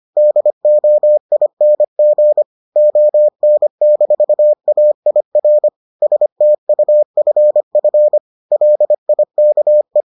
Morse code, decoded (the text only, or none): DOING ON-AIR STUFF LIKE